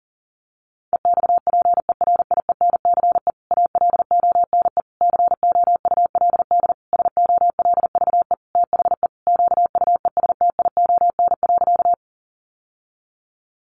Morse code, heard as a keyed tone, about 40 wpm, 700 Hz. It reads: EXPERIENCE ALONE COULD SOLVE THE QUESTION.
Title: Sentences from Top 1000 Words 40wpm